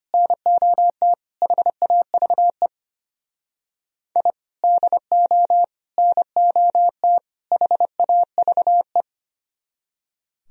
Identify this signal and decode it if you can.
Morse code, heard as a keyed tone, about 32 wpm, 700 Hz. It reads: NOT HAVE I DO NOT HAVE